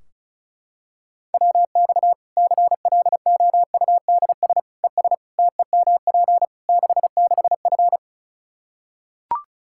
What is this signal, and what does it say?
Morse code, heard as a keyed tone, about 34 wpm, 700 Hz. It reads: WX CLOUDS ES TEMP 66F
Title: Practice QSO - Exchange 2 - Info - 35wpm